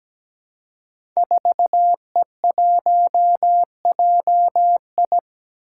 Morse code, heard as a keyed tone, about 17 wpm, 700 Hz. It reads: 4E1JI
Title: Five Challenging Letter-Number Combinations 17wpm